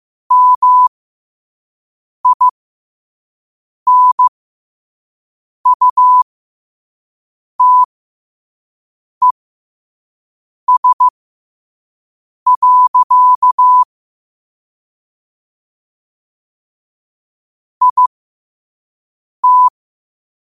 Morse code, heard as a keyed tone, about 15 wpm, spaced as for 5 wpm, 1000 Hz. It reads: MINUTES. IT
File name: qotdfiles / 5wpm